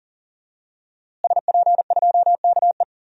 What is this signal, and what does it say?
Morse code, heard as a keyed tone, about 40 wpm, 700 Hz. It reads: SP2KE
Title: DXCC-100 Call Signs - 5 Characters - Repeated 3x - 40wpm